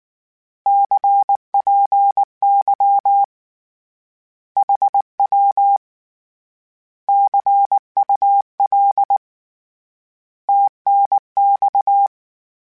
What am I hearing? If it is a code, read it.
Morse code, heard as a keyed tone, about 19 wpm, 800 Hz. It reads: CPY HW CUL TNX
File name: Misc / Misc-mp3s / Words @18